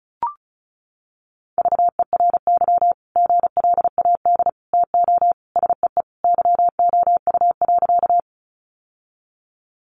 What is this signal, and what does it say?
Morse code, heard as a keyed tone, about 34 wpm, 700 Hz. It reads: VERY GLAD TO SEE YOU.
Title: Sentences from Top 800 Words 35wpm